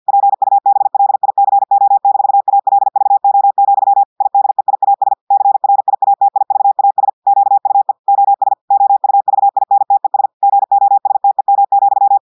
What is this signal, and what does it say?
Morse code, heard as a keyed tone, about 50 wpm, 800 Hz. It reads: PUBLICO-ALVO, IDEIAS CRIATIVAS QUE OS OUVINTES GOSTEM,
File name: portugais texte en clair 2 vitesse 50 mots minutes